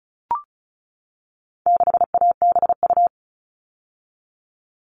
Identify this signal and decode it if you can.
Morse code, heard as a keyed tone, about 34 wpm, 700 Hz. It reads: 6ABU
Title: All Cut Numbers - 4-Digit 35wpm